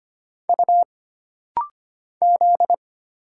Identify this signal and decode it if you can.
Morse code, heard as a keyed tone, about 25 wpm, 700 Hz. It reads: U Z
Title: Single Letters - Rapid-Fire 25wpm